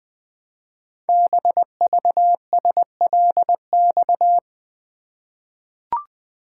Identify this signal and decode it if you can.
Morse code, heard as a keyed tone, about 20 wpm, 700 Hz. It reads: BVSLX
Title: Code Groups - 5 Letter:Numbers - 20wpm